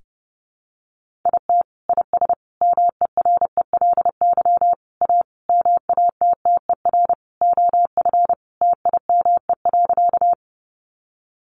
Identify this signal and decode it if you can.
Morse code, heard as a keyed tone, about 30 wpm, 700 Hz. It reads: IT IS MERELY A MATTER OF TIME.
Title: Sentences from Top 1400 Words 30wpm